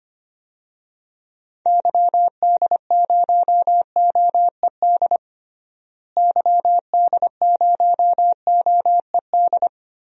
Morse code, heard as a keyed tone, about 25 wpm, 700 Hz. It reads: YD0OED YD0OED
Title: DXCC-100 Call Signs - 6 Characters - Repeated 3x - 25wpm